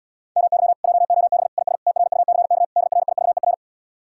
Morse code, heard as a keyed tone, about 77 wpm, 700 Hz. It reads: MY JOB IS TRUCK DRIVER